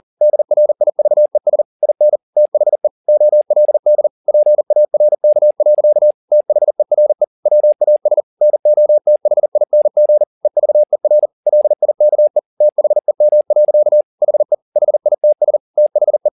Morse code, heard as a keyed tone, about 40 wpm, 600 Hz. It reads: DRIVES IN THE OLD PARK. THERE WAS NOTHING EVER LIKE THEM. HE HITS THE